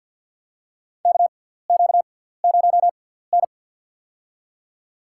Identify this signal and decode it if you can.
Morse code, heard as a keyed tone, about 48 wpm, spaced as for 17 wpm, 700 Hz. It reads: KQ0N
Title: USA Call Signs - 4 Characters - 6x Character Spacing - 50wpm